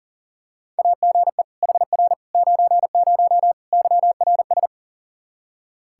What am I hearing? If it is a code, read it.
Morse code, heard as a keyed tone, about 40 wpm, 700 Hz. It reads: AGE HR 90 YRS
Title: Practice QSO - Exchange 3 - More Info - 40wpm